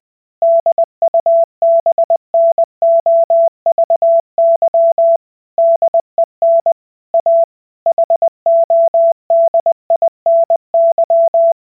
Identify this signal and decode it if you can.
Morse code, heard as a keyed tone, about 20 wpm, 670 Hz. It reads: DUBNOVY DEN A HODINY